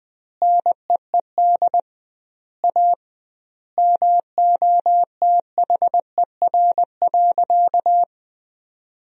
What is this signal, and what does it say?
Morse code, heard as a keyed tone, about 21 wpm, 700 Hz. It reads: NEED A MOTHER.
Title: 2x Word-Spacing - Sentences from Top 200 Words - 20wpm